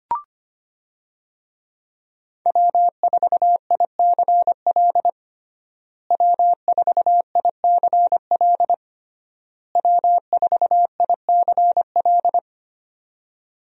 Morse code, heard as a keyed tone, about 25 wpm, 700 Hz. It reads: W4ICL W4ICL W4ICL
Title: USA Call Signs - 5 Characters - Repeated 3x - 25wpm